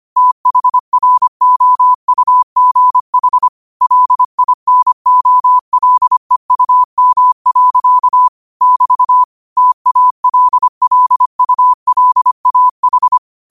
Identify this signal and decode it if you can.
Morse code, heard as a keyed tone, about 25 wpm, 1000 Hz. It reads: THROUGH LINOLEUM. <BT> TALLULAH
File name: qotdfiles / 25wpm